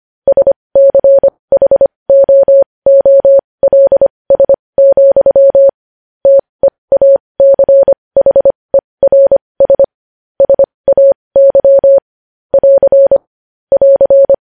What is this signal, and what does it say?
Morse code, heard as a keyed tone, about 25 wpm, spaced as for 20 wpm, 550 Hz. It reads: SCHOOLS, TEACHERS SAY <AR> <AR>
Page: news-headlines-in-morse-code-at-20-wpm